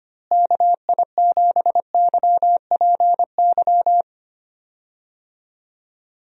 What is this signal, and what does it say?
Morse code, heard as a keyed tone, about 25 wpm, 700 Hz. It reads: KI7YPY
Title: USA Call Signs - 6 Characters - Repeated 2x - 25wpm